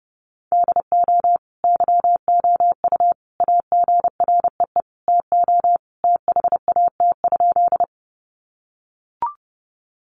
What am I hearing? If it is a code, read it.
Morse code, heard as a keyed tone, about 30 wpm, 700 Hz. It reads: DO YOU AGREE TO THAT?
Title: Sentences from Top 1100 Words 30wpm